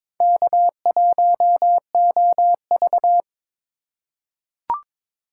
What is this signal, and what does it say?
Morse code, heard as a keyed tone, about 22 wpm, 700 Hz. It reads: K1OV
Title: USA Call Signs - 4 Characters - 2x Character Spacing - 22wpm